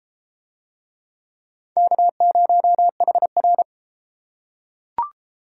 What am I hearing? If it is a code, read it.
Morse code, heard as a keyed tone, about 32 wpm, 700 Hz. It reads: K0HR